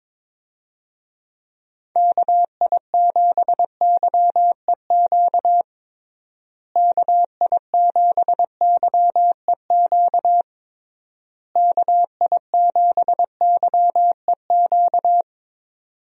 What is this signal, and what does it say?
Morse code, heard as a keyed tone, about 22 wpm, 700 Hz. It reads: KI7YEQ KI7YEQ KI7YEQ
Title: USA Call Signs - 6 Characters - Repeated 3x - 22wpm